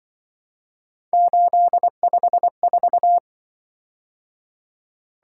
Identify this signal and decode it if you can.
Morse code, heard as a keyed tone, about 24 wpm, 700 Hz. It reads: 854
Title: Speed-Racing - Three Letter-Number Combinations - 20wpm